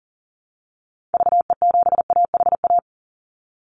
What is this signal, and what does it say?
Morse code, heard as a keyed tone, about 38 wpm, 700 Hz. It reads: VE7AHA
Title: Call Signs Repeated 2x - 40wpm